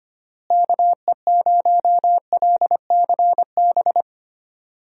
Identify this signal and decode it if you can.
Morse code, heard as a keyed tone, about 25 wpm, 700 Hz. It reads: KE0LCB
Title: USA Call Signs - 6 Characters - Repeated 3x - 25wpm